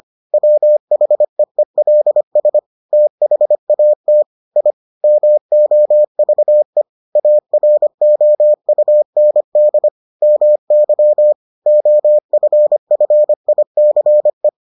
Morse code, heard as a keyed tone, about 25 wpm, 600 Hz. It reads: WHEELS THAT I MOVE AROUND MY OFFICE